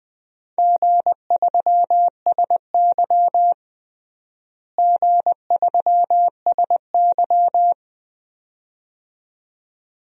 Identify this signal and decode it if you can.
Morse code, heard as a keyed tone, about 20 wpm, 700 Hz. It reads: G3SY G3SY
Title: DXCC-100 Call Signs - 4 Characters - Repeated 2x - 20wpm